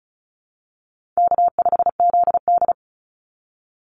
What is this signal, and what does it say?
Morse code, heard as a keyed tone, about 34 wpm, 700 Hz. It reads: K5ZD
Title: Easy Call Signs Repeated 3x - 35wpm